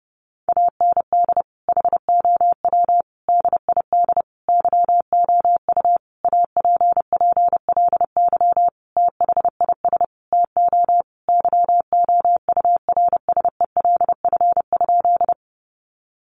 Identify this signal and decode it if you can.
Morse code, heard as a keyed tone, about 30 wpm, 700 Hz. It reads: AND HOW DID YOU APPLY THIS TO YOURSELF?
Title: Sentences from Top 1200 Words 30wpm